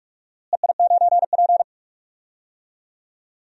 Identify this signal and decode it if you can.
Morse code, heard as a keyed tone, about 47 wpm, 700 Hz. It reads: EI9P
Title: DXCC-100 Call Signs - 4 Characters - Repeated 3x - 45wpm